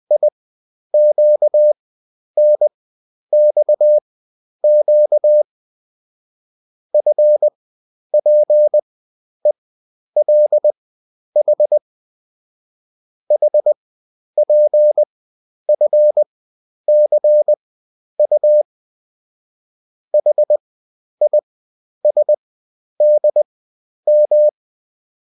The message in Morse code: IQNXQ FPELH HPFCU HISDM